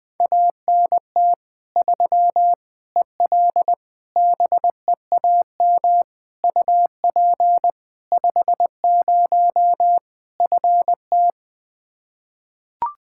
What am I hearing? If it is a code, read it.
Morse code, heard as a keyed tone, about 20 wpm, 700 Hz. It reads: ANT 3 EL BEAM UP 50 FT
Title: Practice QSO - Exchange 2 - Info - 20wpm